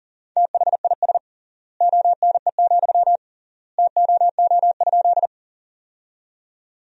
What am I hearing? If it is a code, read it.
Morse code, heard as a keyed tone, about 40 wpm, 700 Hz. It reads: THIS ONE, TOO?
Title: 3x Word-Spacing - Sentences from Top 200 Words - 40wpm